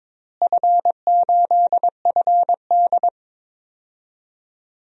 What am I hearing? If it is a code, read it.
Morse code, heard as a keyed tone, about 23 wpm, 700 Hz. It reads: F8FD
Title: DXCC-100 Call Signs - 4 Characters - Repeated 3x - 22wpm